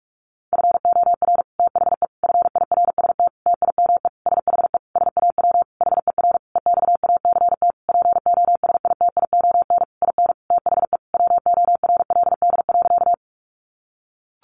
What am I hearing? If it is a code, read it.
Morse code, heard as a keyed tone, about 44 wpm, 700 Hz. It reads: FOR THE FIRST TIME SHE SAW HER EXACT POSITION IN THE WORLD.
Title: Sentences from Top 1300 Words 45wpm